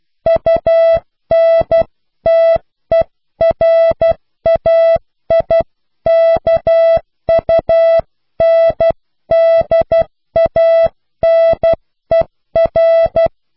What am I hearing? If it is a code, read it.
Morse code, heard as a keyed tone, about 12 wpm, 660 Hz. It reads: UNTERAIKUNDANER
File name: cw / Pseudo woorden (15)